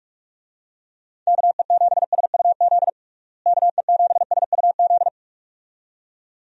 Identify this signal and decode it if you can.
Morse code, heard as a keyed tone, about 46 wpm, 700 Hz. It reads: KE7SUZ KE7SUZ
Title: USA Call Signs - 6 Characters - Repeated 2x - 45wpm